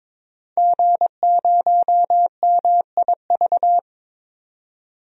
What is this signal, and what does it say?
Morse code, heard as a keyed tone, about 21 wpm, 700 Hz. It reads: G0MIV
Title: DXCC-100 Call Signs - 5 Characters - 22wpm